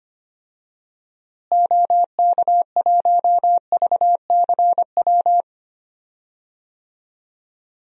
Morse code, heard as a keyed tone, about 25 wpm, 700 Hz. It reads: OK1VCW
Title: DXCC-100 Call Signs - 6 Characters - Repeated 3x - 25wpm